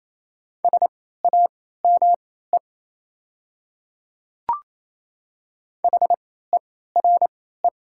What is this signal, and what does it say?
Morse code, heard as a keyed tone, about 28 wpm, spaced as for 16 wpm, 700 Hz. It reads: SAME HERE
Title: 3x Character Spacing - Top 200 Words - 28wpm